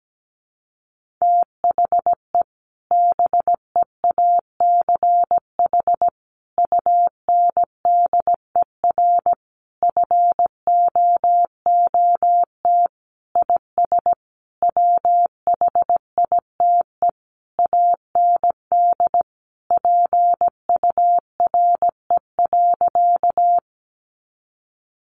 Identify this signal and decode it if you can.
Morse code, heard as a keyed tone, about 17 wpm, 700 Hz. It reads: THE BEACH UNDER FOOT IS WHITE AND PURE.
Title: Sentences from Top 1500 Words 17wpm